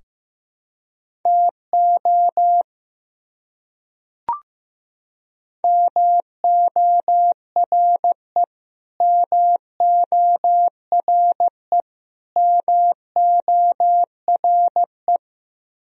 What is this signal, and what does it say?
Morse code, heard as a keyed tone, about 15 wpm, 700 Hz. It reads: TO MORE MORE MORE
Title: Top 100 Words Repeated 15wpm